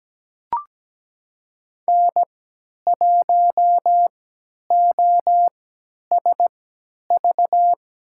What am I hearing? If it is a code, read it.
Morse code, heard as a keyed tone, about 17 wpm, spaced as for 10 wpm, 700 Hz. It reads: N1OSV
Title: USA Call Signs - 5 Characters - 3x Character Spacing - 17wpm